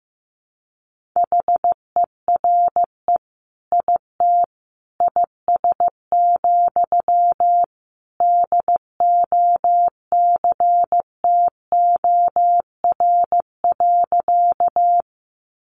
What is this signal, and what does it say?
Morse code, heard as a keyed tone, about 15 wpm, 700 Hz. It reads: HERE IT IS, DOCTOR.
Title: Sentences from Top 1500 Words 15wpm